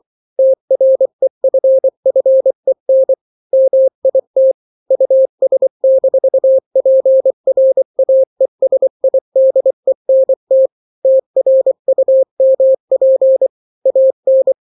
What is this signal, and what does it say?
Morse code, heard as a keyed tone, about 24 wpm, 530 Hz. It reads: TREFFEN MIT US-PRAESIDENT TRUMP AN
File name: daily / 23wpm